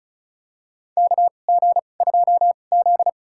Morse code, heard as a keyed tone, about 36 wpm, spaced as for 25 wpm, 700 Hz. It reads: KG2Z
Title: USA Call Signs - 4 Characters - 2x Character Spacing - 35wpm